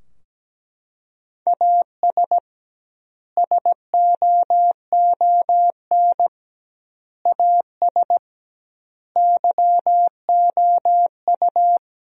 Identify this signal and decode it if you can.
Morse code, heard as a keyed tone, about 17 wpm, 700 Hz. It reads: AS SOON AS YOU
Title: 2x Word-Spacing - Sentences from Top 400 Words - 17wpm